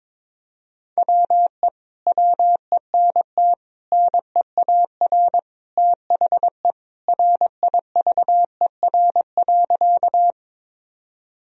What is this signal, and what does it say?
Morse code, heard as a keyed tone, about 22 wpm, 700 Hz. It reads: WE WENT NEAR THE RIVER.
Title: Sentences from Top 1000 Words - Review Part 1 - 22wpm